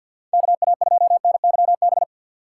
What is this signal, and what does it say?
Morse code, heard as a keyed tone, about 60 wpm, 700 Hz. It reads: KA2NYB
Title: Speed Racing - USA 2x3 Call Signs 50wpm